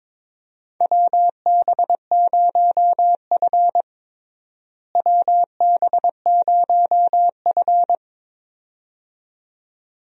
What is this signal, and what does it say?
Morse code, heard as a keyed tone, about 22 wpm, 700 Hz. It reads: WB0F WB0F
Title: USA Call Signs - 4 Characters - Repeated 2x - 22wpm